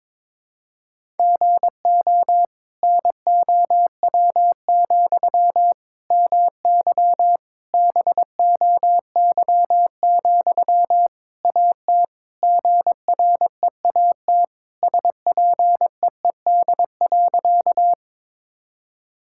Morse code, heard as a keyed tone, about 22 wpm, 700 Hz. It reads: GO NOW, MY BOY, AT GREAT SPEED.
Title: Sentences from Top 1000 Words - Review Part 5 - 22wpm